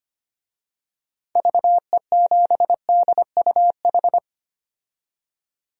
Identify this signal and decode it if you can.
Morse code, heard as a keyed tone, about 25 wpm, 700 Hz. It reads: VE7DUH
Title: DXCC-100 Call Signs - 6 Characters - 25wpm